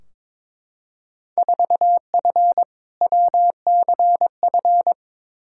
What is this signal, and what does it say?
Morse code, heard as a keyed tone, about 21 wpm, 700 Hz. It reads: 4F WCF
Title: Field Day Exchanges - 22wpm